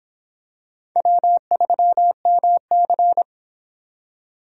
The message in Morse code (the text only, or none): W3MC